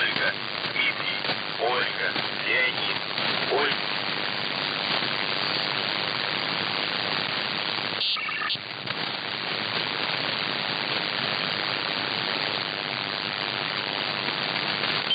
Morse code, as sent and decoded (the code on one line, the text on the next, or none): none